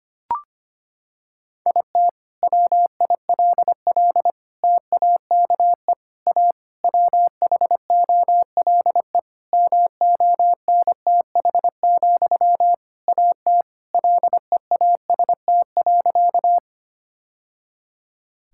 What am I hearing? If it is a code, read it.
Morse code, heard as a keyed tone, about 25 wpm, 700 Hz. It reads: IT WILL TAKE A WHOLE MONTH, AT LEAST.
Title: Sentences from Top 1000 Words - Review Part 5 - 25wpm